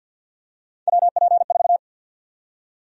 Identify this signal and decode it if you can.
Morse code, heard as a keyed tone, about 53 wpm, 700 Hz. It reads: WP4
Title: USA Call Sign Prefixes - Repeated 2x - 50wpm